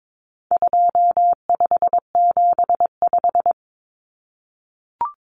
Morse code, heard as a keyed tone, about 22 wpm, 700 Hz. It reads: 2575
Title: Numbers - 4-Digit 22wpm